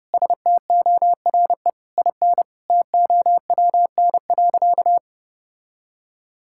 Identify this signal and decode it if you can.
Morse code, heard as a keyed tone, about 30 wpm, 700 Hz. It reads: STORE IN TOWN.
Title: Sentences from Top 1000 Words - Review Part 5 - 30wpm